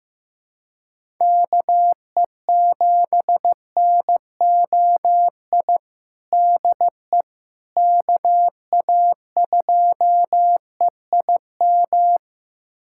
Morse code, heard as a keyed tone, about 15 wpm, 700 Hz. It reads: KE7NOI DE KA2EIM
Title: USA Callsign DE Callsign - 6 Characters - 15wpm